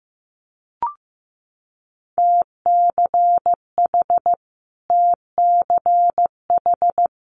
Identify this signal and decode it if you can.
Morse code, heard as a keyed tone, about 15 wpm, 700 Hz. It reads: TCH TCH
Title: Multi-letter Phonemes Repeated 3x 15wpm